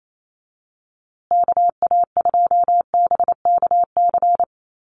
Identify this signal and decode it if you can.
Morse code, heard as a keyed tone, about 28 wpm, 700 Hz. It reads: KA2BKC
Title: USA Call Signs Repeated 3x 28wpm